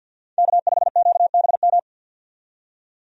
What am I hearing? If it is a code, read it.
Morse code, heard as a keyed tone, about 49 wpm, 700 Hz. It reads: K5QBM
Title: USA Call Signs - 5 Characters - 50wpm